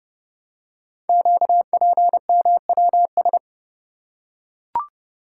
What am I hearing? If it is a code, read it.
Morse code, heard as a keyed tone, about 30 wpm, 700 Hz. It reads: QPMWS
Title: Code Groups - 5 Letters - 30wpm